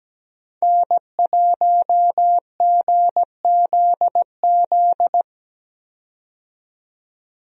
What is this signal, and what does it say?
Morse code, heard as a keyed tone, about 17 wpm, 700 Hz. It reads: N1GZZ
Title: USA Call Signs - 5 Characters - 17wpm